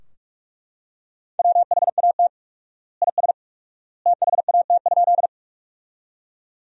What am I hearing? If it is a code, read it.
Morse code, heard as a keyed tone, about 42 wpm, 700 Hz. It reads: WHAT IS THAT?
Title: 4x Word-Spacing - Sentences from Top 100 Words - 45wpm